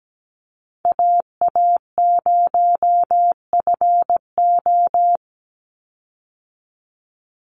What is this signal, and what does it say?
Morse code, heard as a keyed tone, about 17 wpm, 700 Hz. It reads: AA0FO
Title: Call Signs Repeated 3x - 17wpm